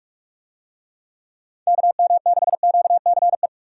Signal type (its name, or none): Morse code